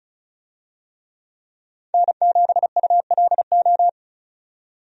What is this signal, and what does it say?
Morse code, heard as a keyed tone, about 34 wpm, 700 Hz. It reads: N7ULO
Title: USA Call Signs - 5 Characters - 35wpm